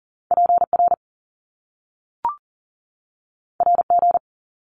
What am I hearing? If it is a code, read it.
Morse code, heard as a keyed tone, about 40 wpm, 700 Hz. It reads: PR RG
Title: Top Two Letter Combinations in Words 40wpm